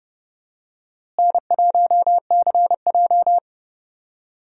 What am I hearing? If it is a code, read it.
Morse code, heard as a keyed tone, about 31 wpm, 700 Hz. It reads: N1CJ
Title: USA Call Signs - 4 Characters - Repeated 2x - 30wpm